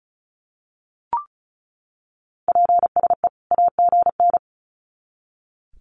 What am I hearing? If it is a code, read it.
Morse code, heard as a keyed tone, about 34 wpm, 700 Hz. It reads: PSE AGN
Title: Common QSO Phrases 35wpm